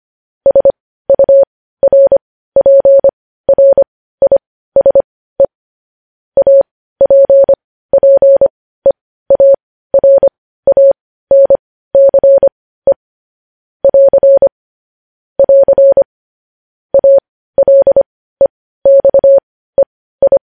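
Morse code, heard as a keyed tone, about 25 wpm, spaced as for 15 wpm, 550 Hz. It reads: SURPRISE APPEARANCE <AR> <AR> ALEXEI
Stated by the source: Morse code news headlines sent at 15 words per minute.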